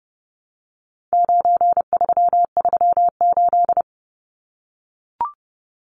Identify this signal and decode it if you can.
Morse code, heard as a keyed tone, about 30 wpm, 700 Hz. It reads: 9338
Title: Numbers - 4-Digit 30wpm